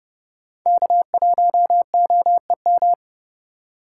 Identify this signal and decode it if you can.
Morse code, heard as a keyed tone, about 30 wpm, 700 Hz. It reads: K1OEM
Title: USA Call Signs - 8x Character Spacing - 30wpm